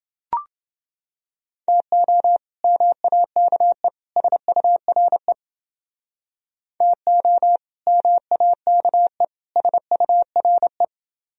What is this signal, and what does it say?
Morse code, heard as a keyed tone, about 29 wpm, 700 Hz. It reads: TO MAKE SURE TO MAKE SURE